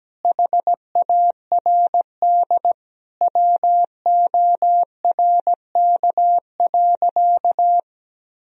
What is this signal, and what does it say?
Morse code, heard as a keyed tone, about 17 wpm, 700 Hz. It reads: HARD WORK.
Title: Sentences from Top 1000 Words - Review Part 1 - 17wpm